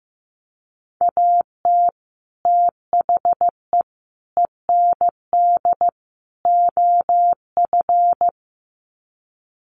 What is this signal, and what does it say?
Morse code, heard as a keyed tone, about 15 wpm, 700 Hz. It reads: AT THE END OF
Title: Sets of 4 Words 15wpm